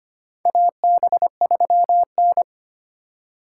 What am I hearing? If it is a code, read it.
Morse code, heard as a keyed tone, about 25 wpm, 700 Hz. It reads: AB3N
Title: USA Call Signs - 4 Characters - 3x Character Spacing - 25wpm